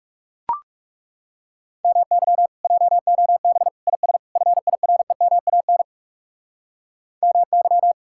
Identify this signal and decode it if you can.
Morse code, heard as a keyed tone, about 40 wpm, 700 Hz. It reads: MY JOB IS FIREMAN MY